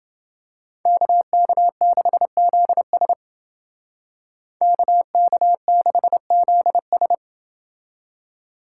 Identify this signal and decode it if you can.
Morse code, heard as a keyed tone, about 29 wpm, 700 Hz. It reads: KK6ZS KK6ZS